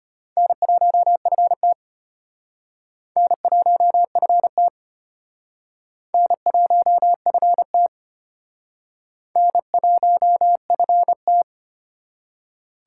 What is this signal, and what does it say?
Morse code, heard as a keyed tone, about 32 wpm, 700 Hz. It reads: N1FT N1FT N1FT N1FT